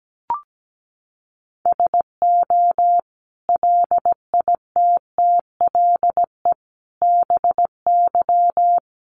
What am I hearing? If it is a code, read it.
Morse code, heard as a keyed tone, about 17 wpm, 700 Hz. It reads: SO LITTLE BY
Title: Sentences from Top 1500 Words 17wpm